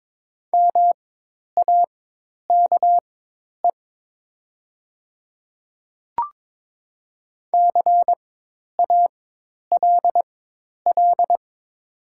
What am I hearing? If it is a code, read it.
Morse code, heard as a keyed tone, about 22 wpm, spaced as for 10 wpm, 700 Hz. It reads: MAKE CALL
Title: 4x Character Spacing - Top 100 Words - 22wpm